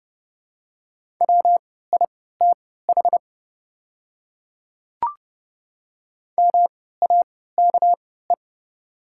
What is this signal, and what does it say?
Morse code, heard as a keyed tone, about 31 wpm, spaced as for 17 wpm, 700 Hz. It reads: WITH MAKE
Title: 3x Character Spacing - Top 100 Words - 30wpm